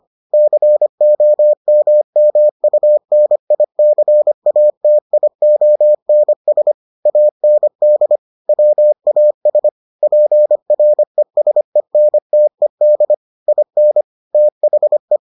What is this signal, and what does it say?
Morse code, heard as a keyed tone, about 25 wpm, 600 Hz. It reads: COMMUNICATIONS AND WAS PRESENTED IN THE